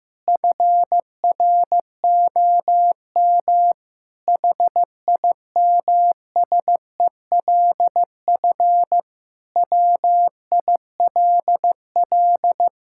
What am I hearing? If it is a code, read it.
Morse code, heard as a keyed tone, about 15 wpm, 700 Hz. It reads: FROM HIMSELF WILL
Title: Sentences from Top 1600 Words 15wpm